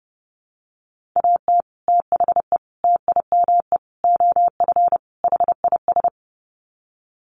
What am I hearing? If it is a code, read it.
Morse code, heard as a keyed tone, about 30 wpm, 700 Hz. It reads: AT THE TIME OF HIS
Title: Sets of 5 Words Encore2 30wpm